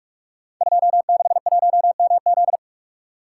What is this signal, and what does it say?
Morse code, heard as a keyed tone, about 44 wpm, 700 Hz. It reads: 261MZ
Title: Code Groups - 5 Letter:Numbers - 45wpm